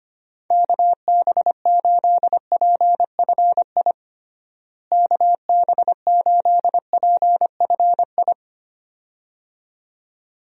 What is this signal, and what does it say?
Morse code, heard as a keyed tone, about 25 wpm, 700 Hz. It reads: KB8PFI KB8PFI
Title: USA Call Signs - 6 Characters - Repeated 2x - 25wpm